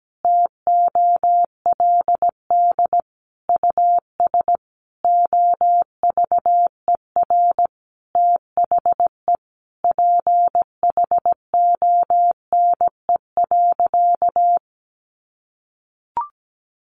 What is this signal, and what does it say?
Morse code, heard as a keyed tone, about 17 wpm, 700 Hz. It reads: TOLD US OVER THE PHONE.
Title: Sentences from Top 1100 Words 17wpm